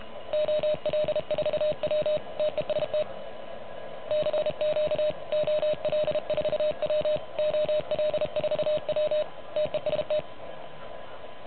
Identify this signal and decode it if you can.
Morse code, heard as a keyed tone, about 30 wpm, 590 Hz. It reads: OL4W TEST CQ OL4W OL4W TEST
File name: truSDX / MWC29.8